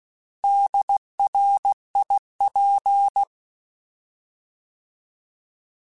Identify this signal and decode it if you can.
Morse code, heard as a keyed tone, about 16 wpm, 800 Hz. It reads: DRIP
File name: Alphabetical / Alphabetical mp3s / Ds @15